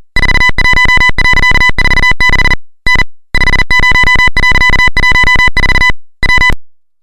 Morse code, heard as a keyed tone, about 40 wpm, 2000 Hz, with noise. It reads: 41.46 N 50.14 W
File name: luxorion / Radio / titanic-40